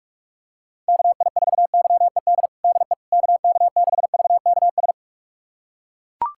Morse code, heard as a keyed tone, about 41 wpm, 700 Hz. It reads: KI4YED DE KK6VKS
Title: USA Callsign DE Callsign - 6 Characters - 45wpm